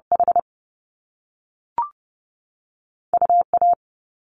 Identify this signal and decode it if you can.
Morse code, heard as a keyed tone, about 30 wpm, 700 Hz. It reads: H UA
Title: Top Two Letter Combinations in Words 30wpm